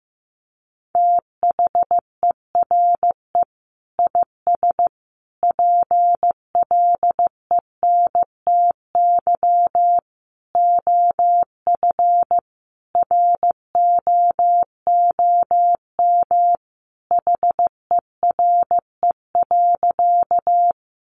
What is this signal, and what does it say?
Morse code, heard as a keyed tone, about 15 wpm, 700 Hz. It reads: THERE IS PLENTY OF ROOM HERE.
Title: Sentences from Top 1300 Words 15wpm